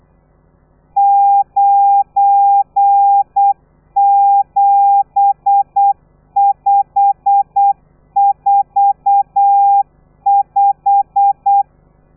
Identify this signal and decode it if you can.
Morse code, heard as a keyed tone, about 8 wpm, 800 Hz. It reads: 97545
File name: telegrafia / i2tfj / Lez12 35